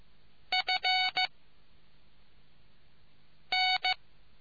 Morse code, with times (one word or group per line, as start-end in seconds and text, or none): 0.5-1.3 F
3.5-3.9 N